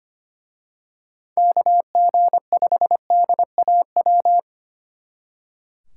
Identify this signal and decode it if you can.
Morse code, heard as a keyed tone, about 27 wpm, 700 Hz. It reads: KG5DAW
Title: Speed Racing - USA 2x3 Call Signs 25wpm